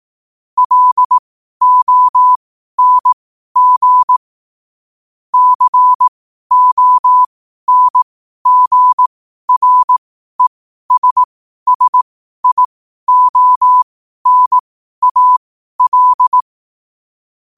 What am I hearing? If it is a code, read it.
Morse code, heard as a keyed tone, about 18 wpm, spaced as for 12 wpm, 1000 Hz. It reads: LONG CONGRESSIONAL
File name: qotdfiles / 12wpm